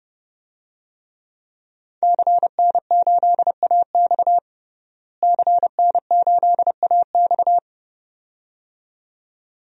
Morse code, heard as a keyed tone, about 30 wpm, 700 Hz. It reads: CN8AX CN8AX
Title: DXCC-100 Call Signs - 5 Characters - Repeated 2x - 30wpm